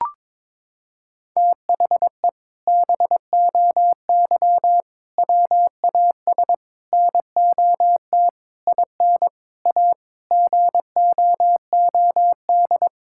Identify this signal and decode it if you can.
Morse code, heard as a keyed tone, about 22 wpm, 700 Hz. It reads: THE BOY WAS NOT IN A GOOD
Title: Sentences from Top 1000 Words - Review Part 6 - 22wpm